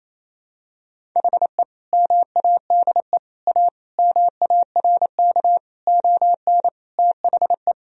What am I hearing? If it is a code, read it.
Morse code, heard as a keyed tone, about 28 wpm, 700 Hz. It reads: HE MADE A MARK ON THE
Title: Sentences from Top 1000 Words - Review Part 1 - 28wpm